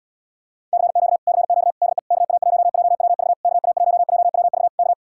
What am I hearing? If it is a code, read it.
Morse code, heard as a keyed tone, about 80 wpm, 700 Hz. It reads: CQ CQ DE KA1JWV KA1JWV K